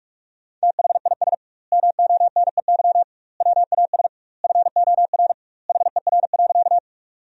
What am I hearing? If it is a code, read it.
Morse code, heard as a keyed tone, about 46 wpm, 700 Hz. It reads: THIS MONEY WAS FOR HER.
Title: 2x Word-Spacing - Sentences from Top 400 Words - 45wpm